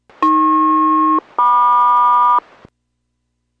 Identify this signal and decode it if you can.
Aircraft SELCAL, heard as two pairs of tones. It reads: AP-MQ